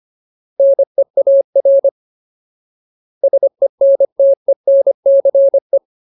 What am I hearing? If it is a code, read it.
Morse code, heard as a keyed tone, about 25 wpm, 550 Hz. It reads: NEAR SENTENCE
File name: Ultimate Word List / 25WPM Ultimate